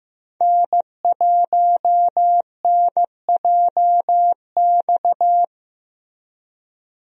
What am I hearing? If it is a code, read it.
Morse code, heard as a keyed tone, about 15 wpm, 700 Hz. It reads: N1NJX
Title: USA Call Signs - 5 Characters - Repeated 2x - 15wpm